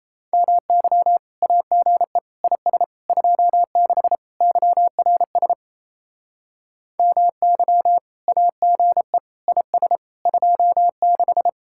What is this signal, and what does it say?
Morse code, heard as a keyed tone, about 35 wpm, 700 Hz. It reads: MY AGE IS 26 YRS MY AGE IS 26